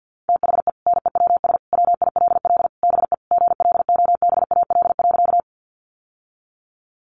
Morse code, heard as a keyed tone, about 50 wpm, 700 Hz. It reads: THE NEWS WILL BE GLOBAL.
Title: Sentences from Top 1200 Words 50wpm